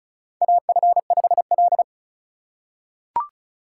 Morse code, heard as a keyed tone, about 34 wpm, 700 Hz. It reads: AF5L
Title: USA Call Signs - 4 Characters - 4x Character Spacing - 35wpm